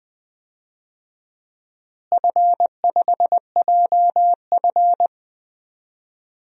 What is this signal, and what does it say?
Morse code, heard as a keyed tone, about 20 wpm, 700 Hz. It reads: F5JF
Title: DXCC-100 Call Signs - 4 Characters - Repeated 3x - 20wpm